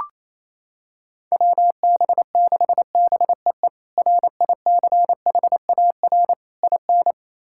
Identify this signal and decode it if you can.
Morse code, heard as a keyed tone, about 28 wpm, 700 Hz. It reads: WB6BEE RICHAR IN
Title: K1USN SST QSO Practice - 28wpm